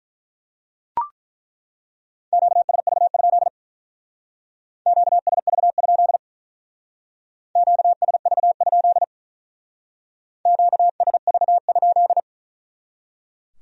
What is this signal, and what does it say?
Morse code, heard as a keyed tone, about 43 wpm, 700 Hz. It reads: QSV? QSV? QSV? QSV?
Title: Speed-Racing Q-Codes - 35wpm